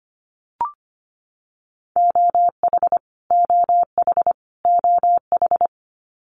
Morse code, heard as a keyed tone, about 25 wpm, 700 Hz. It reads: OH OH OH
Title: US States Repeated 3x 25wpm